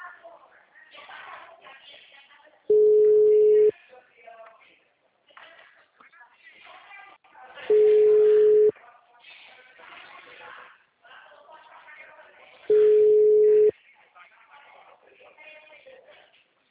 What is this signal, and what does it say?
Ringback tone (Europe, 425 Hz), with speech.